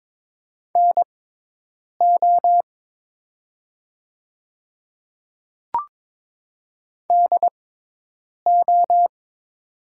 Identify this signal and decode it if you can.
Morse code, heard as a keyed tone, about 22 wpm, spaced as for 8 wpm, 700 Hz. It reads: NO DO
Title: 6x Character Spacing - Top 300 Words - 22wpm